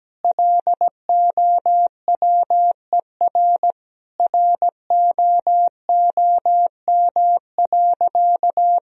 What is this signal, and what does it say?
Morse code, heard as a keyed tone, about 17 wpm, 700 Hz. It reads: LOWER ROOM.
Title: Sentences from Top 1000 Words - Review Part 4 - 17wpm